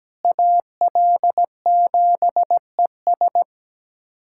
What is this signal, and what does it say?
Morse code, heard as a keyed tone, about 17 wpm, 700 Hz. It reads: AL7ES
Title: USA Callsign DE Callsign - 5 Characters - 17wpm